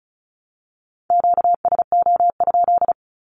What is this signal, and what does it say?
Morse code, heard as a keyed tone, about 34 wpm, 700 Hz. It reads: QSO?
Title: Q-Codes 35wpm